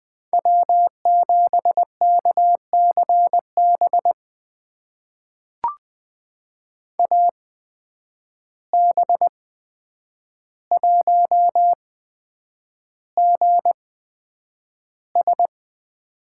Morse code, heard as a keyed tone, about 20 wpm, spaced as for 9 wpm, 700 Hz. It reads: W7KCB AB1GS
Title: USA Call Signs - 8x Character Spacing - 20wpm